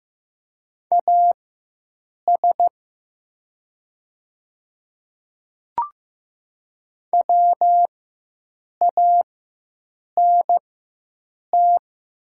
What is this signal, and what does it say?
Morse code, heard as a keyed tone, about 15 wpm, spaced as for 7 wpm, 700 Hz. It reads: AS WANT
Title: 4x Character Spacing - Top 300 Words - 15wpm